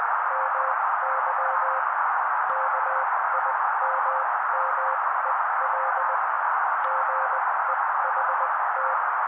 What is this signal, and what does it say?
Morse code, heard as a keyed tone, about 20 wpm, spaced as for 16 wpm, 530 Hz. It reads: MY KIMMEL GEHT
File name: daily / 16wpm noise